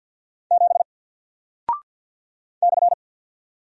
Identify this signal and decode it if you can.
Morse code, heard as a keyed tone, about 48 wpm, 700 Hz. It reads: 7 /
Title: Single Character (Letter:Number:Punctuation:Prosign) Alt 50wpm